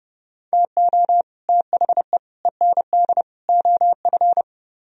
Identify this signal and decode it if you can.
Morse code, heard as a keyed tone, about 30 wpm, 700 Hz. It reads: TO THE END OF